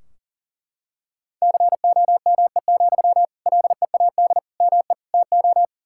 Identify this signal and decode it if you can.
Morse code, heard as a keyed tone, about 40 wpm, 700 Hz. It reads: COME, LEAD ME TO
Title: Sentences from Top 1600 Words 40wpm